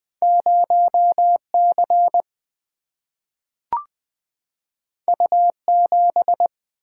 Morse code, heard as a keyed tone, about 20 wpm, 700 Hz. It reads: 0 C U 7
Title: Single-Letter Single-Number 20wpm